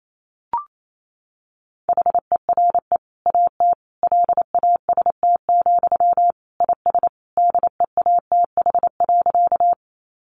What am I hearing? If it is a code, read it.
Morse code, heard as a keyed tone, about 28 wpm, 700 Hz. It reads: HERE AT LAST, IS DEATH.
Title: Sentences from Top 1100 Words 28wpm